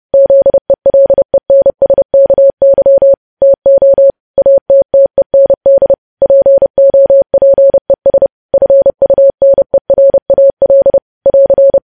Morse code, heard as a keyed tone, about 30 wpm, 550 Hz. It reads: ZELENSKY TO ATTEND POPES FUNERAL <AR>
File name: podcast / News headlines-1x-30x30